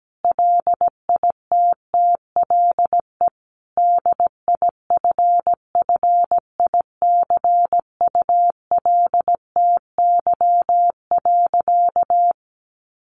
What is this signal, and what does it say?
Morse code, heard as a keyed tone, about 17 wpm, 700 Hz. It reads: LITTLE DIFFICULTY.
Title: Sentences from Top 1500 Words 17wpm